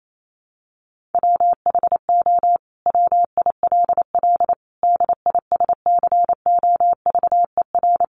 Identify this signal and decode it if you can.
Morse code, heard as a keyed tone, about 28 wpm, 700 Hz. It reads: WHO WILL DISCOVER
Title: Sentences from Top 1100 Words 28wpm